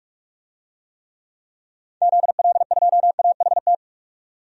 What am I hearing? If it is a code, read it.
Morse code, heard as a keyed tone, about 42 wpm, 700 Hz. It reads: ZL2AHT
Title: DXCC-100 Call Signs - 6 Characters - 45wpm